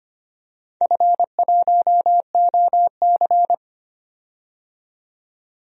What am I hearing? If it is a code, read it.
Morse code, heard as a keyed tone, about 25 wpm, 700 Hz. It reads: F1OC
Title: DXCC-100 Call Signs - 4 Characters - 25wpm